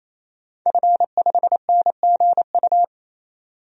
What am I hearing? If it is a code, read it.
Morse code, heard as a keyed tone, about 28 wpm, 700 Hz. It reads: F5NGU
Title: DXCC-100 Call Signs - 5 Characters - 28wpm